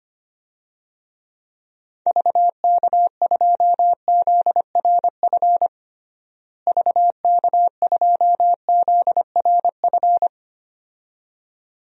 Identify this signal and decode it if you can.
Morse code, heard as a keyed tone, about 25 wpm, 700 Hz. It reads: VK2ZRF VK2ZRF
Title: DXCC-100 Call Signs - 6 Characters - Repeated 2x - 25wpm